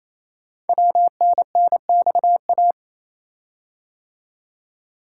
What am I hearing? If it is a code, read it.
Morse code, heard as a keyed tone, about 28 wpm, 700 Hz. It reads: WNNXA
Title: Code Groups - 5 Letters - 28wpm